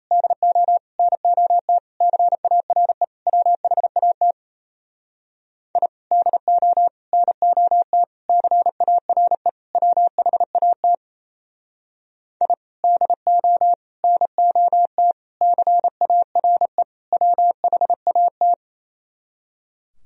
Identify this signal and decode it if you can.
Morse code, heard as a keyed tone, about 37 wpm, 700 Hz. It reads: DO NOT CARE WHAT I DO NOT CARE WHAT I DO NOT CARE WHAT